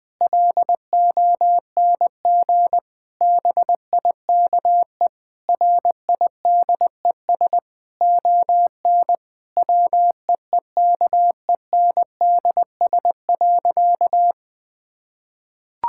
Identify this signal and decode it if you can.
Morse code, heard as a keyed tone, about 20 wpm, 700 Hz. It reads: LONG BIKE RIDES ON WEEKENDS.
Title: Difficult Sentences with Hints - 20wpm